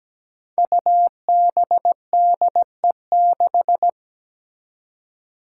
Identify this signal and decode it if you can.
Morse code, heard as a keyed tone, about 17 wpm, 700 Hz. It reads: UBDE6
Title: Five Challenging Letter-Number Combinations 17wpm